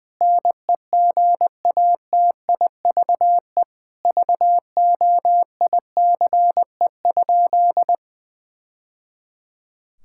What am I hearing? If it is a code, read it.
Morse code, heard as a keyed tone, about 20 wpm, 700 Hz. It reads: NEGATIVE VOICE?
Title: Sentences from Top 1000 Words - Review Part 6 - 20wpm